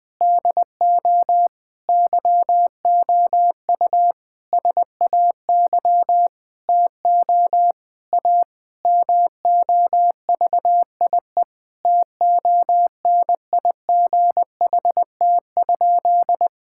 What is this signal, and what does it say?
Morse code, heard as a keyed tone, about 21 wpm, 700 Hz. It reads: DO YOU SAY TO A MOVIE TONIGHT?
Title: Sentences with 5-Word Phrases - 20wpm